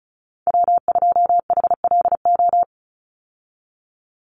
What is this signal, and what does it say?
Morse code, heard as a keyed tone, about 34 wpm, 700 Hz. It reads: W2HLO
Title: USA Call Signs 35wpm